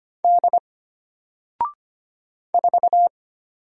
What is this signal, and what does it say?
Morse code, heard as a keyed tone, about 25 wpm, 700 Hz. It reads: D 4
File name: Single-Letter-Number-25wpm